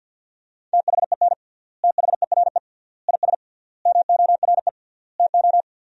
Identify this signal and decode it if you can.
Morse code, heard as a keyed tone, about 50 wpm, 700 Hz. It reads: THEN THERE IS MORE TO
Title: 3x Word-Spacing - Sentences from Top 100 Words - 50wpm